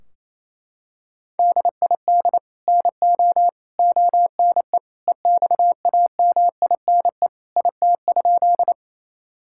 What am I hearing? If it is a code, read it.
Morse code, heard as a keyed tone, about 28 wpm, 700 Hz. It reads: DID NO ONE EXAMINE IT?
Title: Sentences from Top 1600 Words 28wpm